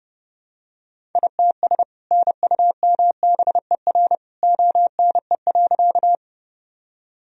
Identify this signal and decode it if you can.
Morse code, heard as a keyed tone, about 32 wpm, 700 Hz. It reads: ITS NUMBER ONE.
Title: Sentences from Top 1000 Words - Review Part 1 - 30wpm